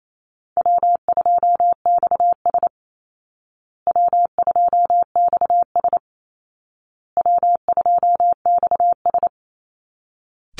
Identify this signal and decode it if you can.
Morse code, heard as a keyed tone, about 28 wpm, 700 Hz. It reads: W2XS W2XS W2XS
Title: Easy Call Signs Repeated 3x - 28wpm